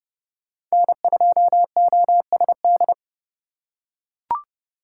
Morse code, heard as a keyed tone, about 30 wpm, 700 Hz. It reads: N2OSD
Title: USA Call Signs - 5 Characters - 2x Character Spacing - 30wpm